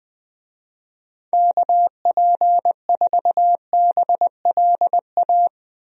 Morse code, heard as a keyed tone, about 21 wpm, 700 Hz. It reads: KP4BLA
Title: DXCC-100 Call Signs - 6 Characters - Repeated 3x - 20wpm